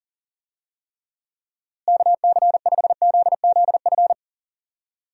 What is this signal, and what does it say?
Morse code, heard as a keyed tone, about 37 wpm, 700 Hz. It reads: KC5ZZF
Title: USA Call Signs - 6 Characters - Repeated 3x - 40wpm